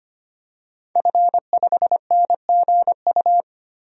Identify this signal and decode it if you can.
Morse code, heard as a keyed tone, about 25 wpm, 700 Hz. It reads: F5NGU
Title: DXCC-100 Call Signs - 5 Characters - 25wpm